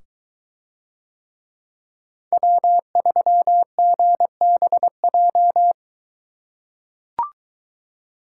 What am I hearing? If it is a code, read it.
Morse code, heard as a keyed tone, about 21 wpm, 700 Hz. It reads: W3GBJ